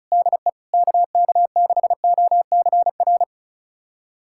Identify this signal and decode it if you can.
Morse code, heard as a keyed tone, about 34 wpm, 700 Hz. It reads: DE KK6OCR
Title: USA Callsign DE Callsign 35wpm